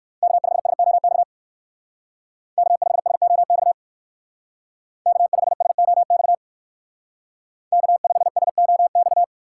Morse code, heard as a keyed tone, about 58 wpm, 700 Hz. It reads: K5SOX K5SOX K5SOX K5SOX